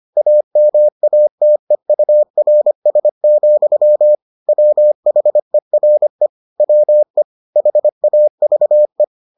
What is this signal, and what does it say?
Morse code, heard as a keyed tone, about 25 wpm, 600 Hz. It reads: AMATEURS, WHERE WE HAVE